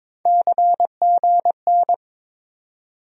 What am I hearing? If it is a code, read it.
Morse code, heard as a keyed tone, about 22 wpm, 700 Hz. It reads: CGN
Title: USA Call Sign Postfixes - 3 Letters 22wpm